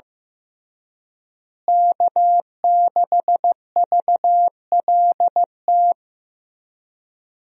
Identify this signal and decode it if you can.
Morse code, heard as a keyed tone, about 15 wpm, 700 Hz. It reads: K6VLT
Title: USA Call Signs - 5 Characters - Repeated 3x - 15wpm